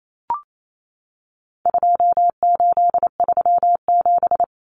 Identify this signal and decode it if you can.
Morse code, heard as a keyed tone, about 28 wpm, 700 Hz. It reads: 2837
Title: Common Cut Numbers - 4-Digit 28wpm